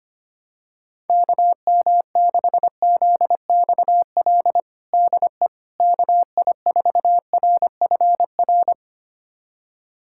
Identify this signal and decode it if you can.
Morse code, heard as a keyed tone, about 25 wpm, 700 Hz. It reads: KM6ZXL DE KI4RFR
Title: USA Callsign DE Callsign 25wpm